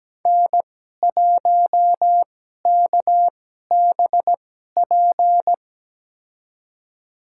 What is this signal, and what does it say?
Morse code, heard as a keyed tone, about 17 wpm, spaced as for 12 wpm, 700 Hz. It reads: N1KBP
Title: USA Call Signs - 5 Characters - 2x Character Spacing - 17wpm